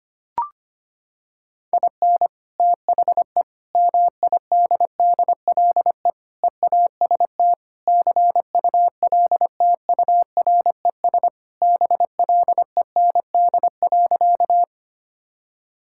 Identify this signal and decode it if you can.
Morse code, heard as a keyed tone, about 25 wpm, 700 Hz. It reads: IN THE MIDDLE EAST CULTURES BLEND.
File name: practice / 25wpm / Sentences with 4-Word Phrases - 25wpm